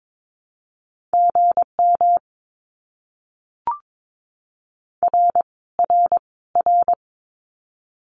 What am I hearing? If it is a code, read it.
Morse code, heard as a keyed tone, about 22 wpm, 700 Hz. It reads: GM R R R
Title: Abbreviations Repeated 3x - 22wpm